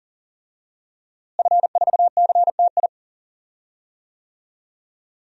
Morse code, heard as a keyed tone, about 40 wpm, 700 Hz. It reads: F4CTI
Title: DXCC-100 Call Signs - 5 Characters - Repeated 3x - 40wpm